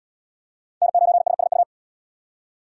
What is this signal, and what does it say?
Morse code, heard as a keyed tone, about 68 wpm, 700 Hz. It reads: N9SSU